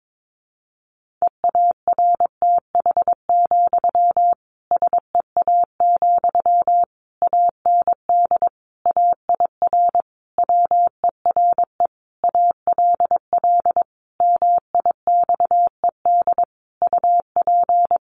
Morse code, heard as a keyed tone, about 22 wpm, 700 Hz. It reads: EARTH, SEA, AND AIR WERE ALL MIXED UP
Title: Sentences from Top 1500 Words 22wpm